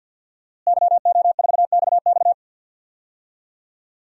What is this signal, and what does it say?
Morse code, heard as a keyed tone, about 52 wpm, 700 Hz. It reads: YO4XX
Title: DXCC-100 Call Signs - 5 Characters - Repeated 3x - 50wpm